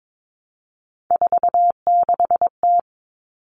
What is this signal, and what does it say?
Morse code, heard as a keyed tone, about 22 wpm, 700 Hz. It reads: 46T
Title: All Cut Numbers - 3-Digit 22wpm